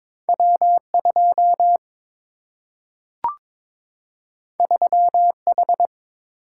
Morse code, heard as a keyed tone, about 22 wpm, 700 Hz. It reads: W2 3H
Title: Single-Letter Single-Number 22wpm